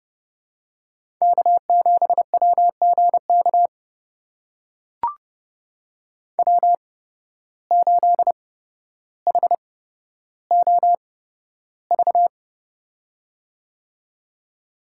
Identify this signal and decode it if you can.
Morse code, heard as a keyed tone, about 30 wpm, spaced as for 13 wpm, 700 Hz. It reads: K7WGK W8HOV
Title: USA Call Signs - 5 Characters - 8x Character Spacing - 30wpm